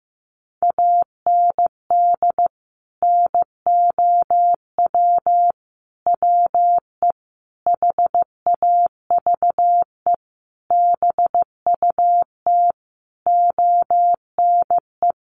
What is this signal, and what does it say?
Morse code, heard as a keyed tone, about 15 wpm, 700 Hz. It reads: AND NOW WE HAVE BUT ONE
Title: Sentences from Top 1500 Words 15wpm